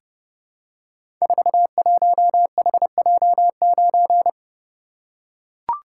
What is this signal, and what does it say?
Morse code, heard as a keyed tone, about 30 wpm, 700 Hz. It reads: 41HJ9
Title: Code Groups - 5 Letter:Numbers - 30wpm